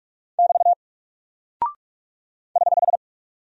Morse code, heard as a keyed tone, about 44 wpm, 700 Hz. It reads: <BT> <HH>
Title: Single Character (Letter:Number:Punctuation:Prosign) 45wpm